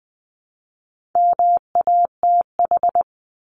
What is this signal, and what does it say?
Morse code, heard as a keyed tone, about 20 wpm, 700 Hz. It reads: MATH
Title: Top 751-1000 Words 20wpm